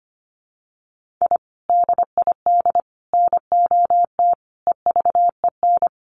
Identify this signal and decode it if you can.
Morse code, heard as a keyed tone, about 25 wpm, 700 Hz. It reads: I DID NOT EVEN
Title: Sets of 5 Words Encore 25wpm